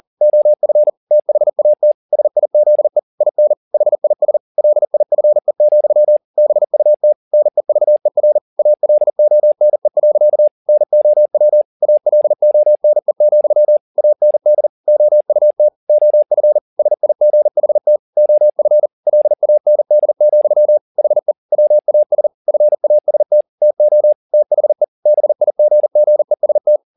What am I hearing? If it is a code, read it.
Morse code, heard as a keyed tone, about 40 wpm, 600 Hz. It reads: OF THAT SIZE IN HIS LIFE, BUT NEVER ALONE. NOW ALONE, AND OUT OF SIGHT OF LAND, HE WAS FAST TO THE BIGGEST